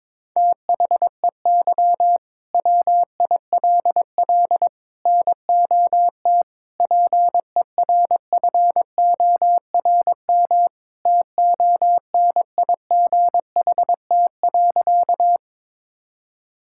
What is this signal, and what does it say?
Morse code, heard as a keyed tone, about 22 wpm, 700 Hz. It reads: THEY WILL NOT PERFORM TONIGHT.
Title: Sentences from Top 1000 Words - Review Part 5 - 22wpm